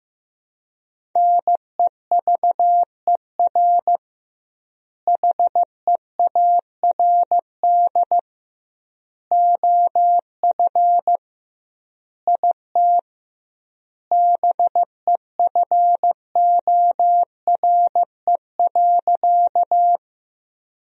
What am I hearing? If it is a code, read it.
Morse code, heard as a keyed tone, about 15 wpm, 700 Hz. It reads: NEVER HEARD OF IT BEFORE.
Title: 2x Word-Spacing - Sentences from Top 400 Words - 15wpm